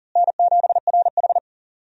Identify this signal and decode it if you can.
Morse code, heard as a keyed tone, about 40 wpm, 700 Hz. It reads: N7RH
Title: USA Call Signs - 4 Characters - Repeated 3x - 40wpm